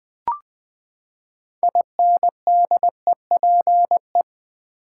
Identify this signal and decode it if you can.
Morse code, heard as a keyed tone, about 20 wpm, 700 Hz. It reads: INDEPE
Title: Build A Word - 6 Letters At A Time - 12-Letter Words - 20wpm